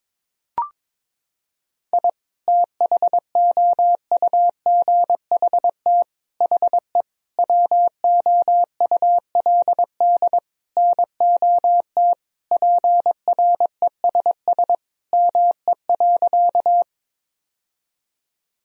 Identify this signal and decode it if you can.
Morse code, heard as a keyed tone, about 22 wpm, 700 Hz. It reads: I THOUGHT HE WOULD NOT PRESS ME.
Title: Sentences from Top 1000 Words - Review Part 1 - 22wpm